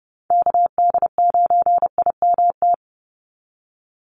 Morse code, heard as a keyed tone, about 30 wpm, 700 Hz. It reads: KD9IMT
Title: USA Call Signs 30wpm